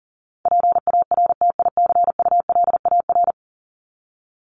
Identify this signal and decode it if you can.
Morse code, heard as a keyed tone, about 40 wpm, 700 Hz. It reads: PARTICULAR
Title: LY Suffix - Top 100 Words with Base Words 40wpm